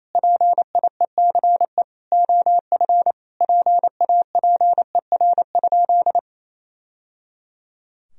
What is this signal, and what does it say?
Morse code, heard as a keyed tone, about 28 wpm, 700 Hz. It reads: PIECE OF PAPER?
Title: Sentences from Top 1000 Words - Review Part 2 - 28wpm